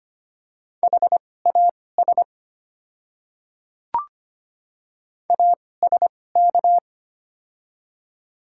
Morse code, heard as a keyed tone, about 25 wpm, spaced as for 18 wpm, 700 Hz. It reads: HAS ASK
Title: 2x Character Spacing - Top 200 Words - 25wpm